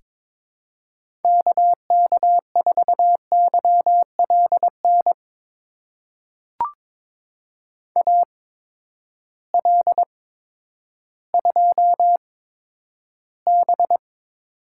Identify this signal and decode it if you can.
Morse code, heard as a keyed tone, about 22 wpm, spaced as for 11 wpm, 700 Hz. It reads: KK4YLN AL2B
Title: USA Call Signs - 8x Character Spacing - 22wpm